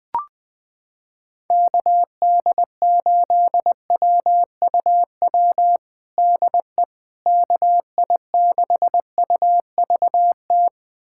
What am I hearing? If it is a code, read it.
Morse code, heard as a keyed tone, about 21 wpm, 700 Hz. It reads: KD8WUW DE KI6UVT
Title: USA Callsign DE Callsign - 6 Characters - 20wpm